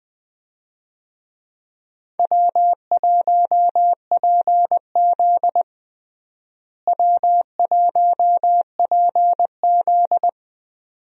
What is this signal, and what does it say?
Morse code, heard as a keyed tone, about 20 wpm, 700 Hz. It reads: W1PZ W1PZ
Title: USA Call Signs - 4 Characters - Repeated 2x - 20wpm